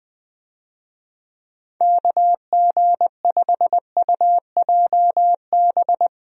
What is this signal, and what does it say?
Morse code, heard as a keyed tone, about 21 wpm, 700 Hz. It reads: KG5UJB
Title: USA Call Signs - 6 Characters - Repeated 3x - 20wpm